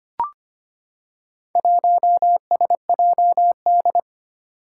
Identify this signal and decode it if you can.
Morse code, heard as a keyed tone, about 25 wpm, 700 Hz. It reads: 1SJD
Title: Four Challenging Letter-Number Combinations 25wpm